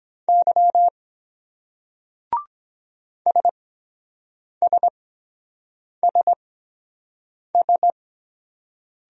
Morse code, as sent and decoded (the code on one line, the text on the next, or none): -.-- / ... ... ... ...
Y SSSS